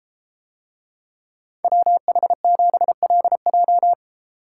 Morse code, heard as a keyed tone, about 33 wpm, 700 Hz. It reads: WH7LJ